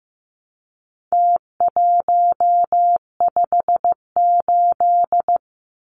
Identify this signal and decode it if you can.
Morse code, heard as a keyed tone, about 15 wpm, 700 Hz. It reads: T158
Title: Numbers - Contest Style - 4-Digit 15wpm